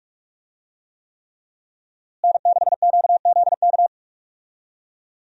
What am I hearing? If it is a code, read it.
Morse code, heard as a keyed tone, about 41 wpm, 700 Hz. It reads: N6QZK
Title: USA Call Signs - 5 Characters - Repeated 3x - 45wpm